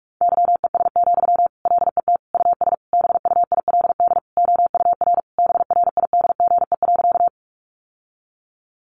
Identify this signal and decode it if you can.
Morse code, heard as a keyed tone, about 44 wpm, 700 Hz. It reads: YES, LET US BUILD OUR BRIDGE.
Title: Sentences from Top 1200 Words 45wpm